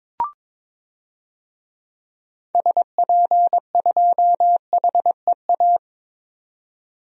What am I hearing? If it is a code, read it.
Morse code, heard as a keyed tone, about 22 wpm, 700 Hz. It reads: SP2HEA
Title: DXCC-100 Call Signs - 6 Characters - 22wpm